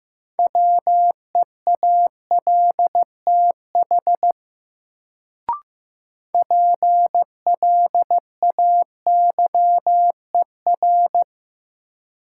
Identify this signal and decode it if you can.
Morse code, heard as a keyed tone, about 15 wpm, 700 Hz. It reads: WEALTH PLAYER
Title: Sentences from Top 1000 Words - Review Part 4 - New Word Drill - 15wpm